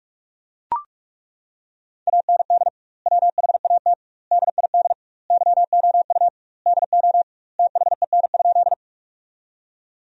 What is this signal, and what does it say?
Morse code, heard as a keyed tone, about 41 wpm, 700 Hz. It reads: AND WHAT DID YOU DO THEN?
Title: 2x Word-Spacing - Sentences from Top 100 Words - 45wpm